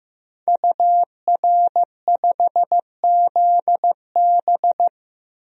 Morse code, heard as a keyed tone, about 15 wpm, 700 Hz. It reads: UR5ZB
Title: DXCC-100 Call Signs - 5 Characters - 15wpm